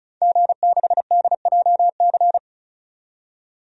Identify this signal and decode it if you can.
Morse code, heard as a keyed tone, about 34 wpm, 700 Hz. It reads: G6DJC
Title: DXCC-100 Call Signs - 5 Characters - 35wpm